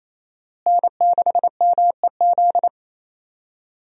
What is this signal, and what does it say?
Morse code, heard as a keyed tone, about 27 wpm, 700 Hz. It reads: N6MEZ
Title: USA Call Signs - 5 Characters - Repeated 2x - 28wpm